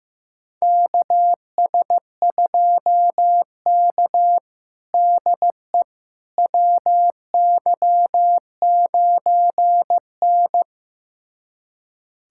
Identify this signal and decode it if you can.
Morse code, heard as a keyed tone, about 15 wpm, 700 Hz. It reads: KS2K DE WY9N
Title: USA Callsign DE Callsign - 4 Characters - 15wpm